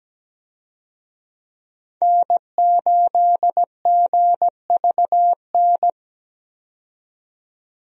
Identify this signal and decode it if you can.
Morse code, heard as a keyed tone, about 17 wpm, 700 Hz. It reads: N8GVN
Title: USA Call Signs - 5 Characters - 17wpm